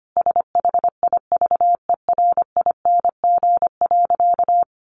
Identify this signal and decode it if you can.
Morse code, heard as a keyed tone, about 25 wpm, 700 Hz. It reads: SHIVERING.
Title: A Sound Of Thunder 25wpm